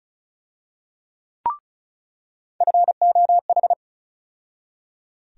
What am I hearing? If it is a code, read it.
Morse code, heard as a keyed tone, about 36 wpm, 700 Hz. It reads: FOH
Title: USA Call Sign Postfixes - 3 Letters 35wpm